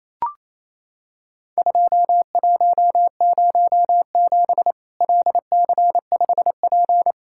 Morse code, heard as a keyed tone, about 27 wpm, 700 Hz. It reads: 2107 LC5P
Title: Worked All Europe DX Contest - QTC Exchange - 28wpm